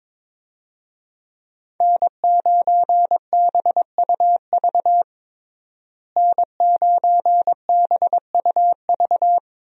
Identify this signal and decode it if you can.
Morse code, heard as a keyed tone, about 22 wpm, 700 Hz. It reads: N9BUV N9BUV
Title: USA Call Signs - 5 Characters - Repeated 3x - 22wpm